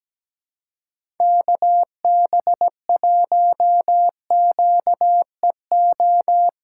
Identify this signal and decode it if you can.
Morse code, heard as a keyed tone, about 17 wpm, 700 Hz. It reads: KB1QEO
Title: USA Call Signs - 6x Character Spacing - 17wpm